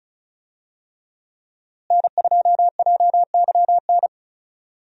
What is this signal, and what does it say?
Morse code, heard as a keyed tone, about 36 wpm, 700 Hz. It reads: N2JYN
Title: USA Call Signs - 5 Characters - Repeated 3x - 35wpm